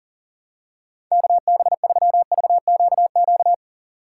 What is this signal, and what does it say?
Morse code, heard as a keyed tone, about 40 wpm, 700 Hz. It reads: KB3VQQ
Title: USA Call Signs - 6 Characters - 6x Character Spacing - 40wpm